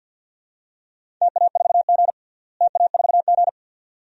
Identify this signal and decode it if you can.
Morse code, heard as a keyed tone, about 51 wpm, 700 Hz. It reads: TA4G TA4G
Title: DXCC-100 Call Signs - 4 Characters - Repeated 2x - 50wpm